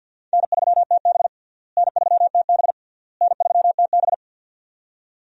Morse code, heard as a keyed tone, about 51 wpm, 700 Hz. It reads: N3TB N3TB N3TB
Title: USA Call Signs - 4 Characters - Repeated 3x - 50wpm